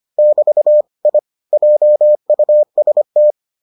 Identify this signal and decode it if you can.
Morse code, heard as a keyed tone, about 25 wpm, 600 Hz. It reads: <BT> I JUST